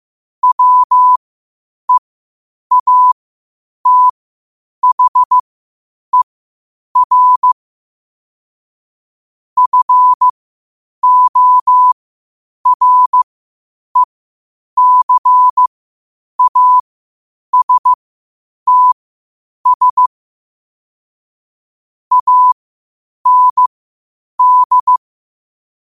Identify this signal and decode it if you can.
Morse code, heard as a keyed tone, about 15 wpm, spaced as for 8 wpm, 1000 Hz. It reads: WEATHER FORECASTS AND
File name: qotdfiles / 8wpm